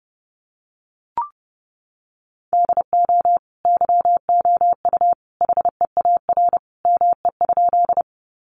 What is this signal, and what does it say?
Morse code, heard as a keyed tone, about 30 wpm, 700 Hz. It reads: DO YOU HEAR ME?
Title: Sentences from Top 300 Words 30wpm